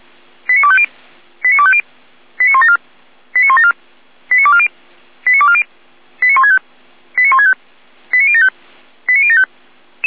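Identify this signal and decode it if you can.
Five-tone selective call (ZVEI), heard as quick runs of five tones, tones of about 70 ms. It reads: 89250 89250 89174 89174 89240 89240 89165 89165 89075 89075
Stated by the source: ZVEI five-tone selective call 89337 received on SDR.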